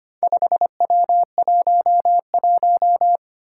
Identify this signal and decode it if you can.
Morse code, heard as a keyed tone, about 25 wpm, 700 Hz. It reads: 5W11
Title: Four Challenging Letter-Number Combinations 25wpm